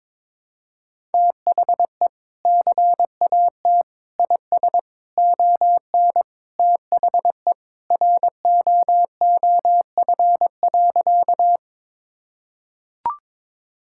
Morse code, heard as a keyed tone, about 22 wpm, 700 Hz. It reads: THE CAT IS ON THE ROOF.
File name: practice / 22wpm / Sentences with 3-Word Phrases - 22wpm